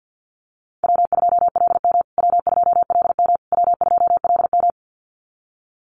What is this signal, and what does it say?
Morse code, heard as a keyed tone, about 48 wpm, 700 Hz. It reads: W2LM W2LM W2LM
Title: USA Call Signs Repeated 3x 50wpm